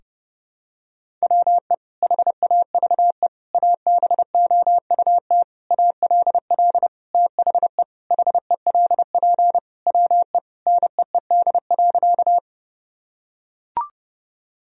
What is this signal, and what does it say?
Morse code, heard as a keyed tone, about 30 wpm, 700 Hz. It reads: WE HAVE ABOUT ALL THE HELP WE NEED.
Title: Sentences from Top 1000 Words - Review Part 1 - 30wpm